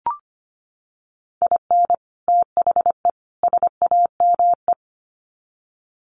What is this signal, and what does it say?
Morse code, heard as a keyed tone, about 25 wpm, 700 Hz. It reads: IN THE SAME
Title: Sets of 3 Words 25wpm